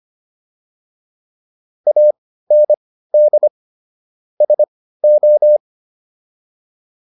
Morse code, heard as a keyed tone, about 25 wpm, spaced as for 15 wpm, 600 Hz. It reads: AND SO